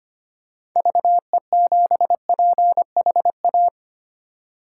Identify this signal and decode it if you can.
Morse code, heard as a keyed tone, about 25 wpm, 700 Hz. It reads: VE7PHA
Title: K1USN SST QSO Practice - 25wpm